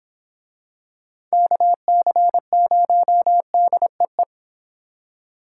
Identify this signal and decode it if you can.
Morse code, heard as a keyed tone, about 26 wpm, 700 Hz. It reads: KC0DEE